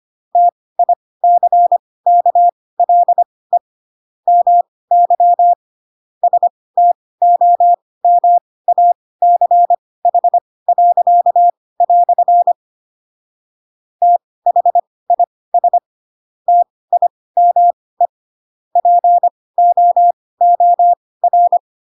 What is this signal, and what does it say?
Morse code, heard as a keyed tone, about 25 wpm, spaced as for 18 wpm, 700 Hz. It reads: TICKLE MY STOMACH." THIS TIME POOR
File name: Pinocchio / Pinocchio ch1 12 25 ik1hge